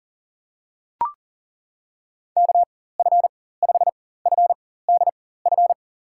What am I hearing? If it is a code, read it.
Morse code, heard as a keyed tone, about 37 wpm, spaced as for 18 wpm, 700 Hz. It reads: KF5FDF
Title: USA Call Signs - 6 Characters - 4x Character Spacing - 40wpm